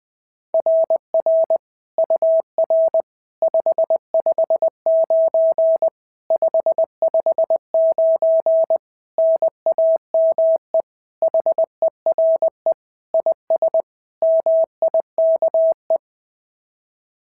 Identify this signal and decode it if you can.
Morse code, heard as a keyed tone, about 20 wpm, 690 Hz. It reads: RR UR 559 559 NAME HERE IS MIKE
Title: SOTA QSO Practice 20wpm